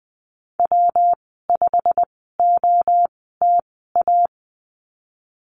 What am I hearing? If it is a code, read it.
Morse code, heard as a keyed tone, about 20 wpm, spaced as for 15 wpm, 700 Hz. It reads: W5OTA
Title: 128 Lesson 21g - Number 6 - Call Signs 20wpm